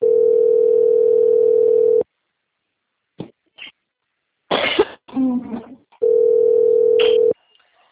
Ringback tone (North America), 440 + 480 Hz.
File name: out- - - - - .74191